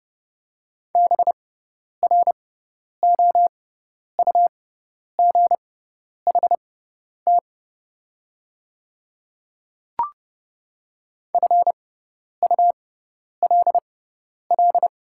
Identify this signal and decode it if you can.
Morse code, heard as a keyed tone, about 30 wpm, spaced as for 10 wpm, 700 Hz. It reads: BROUGHT FULL
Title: 6x Character Spacing - Top 500 Words - 30wpm